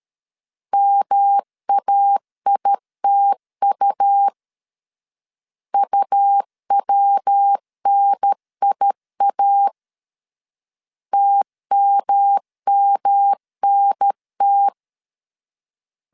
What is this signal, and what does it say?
Morse code, heard as a keyed tone, about 13 wpm, 800 Hz. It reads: MAITU UWNIA TMMNT
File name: CW / DET / 1 3 1500C S B 12wpm 2828